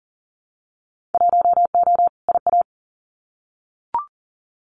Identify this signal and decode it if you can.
Morse code, heard as a keyed tone, about 40 wpm, 700 Hz. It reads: 1O IA
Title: Winter Field Day Exchanges Encore 40wpm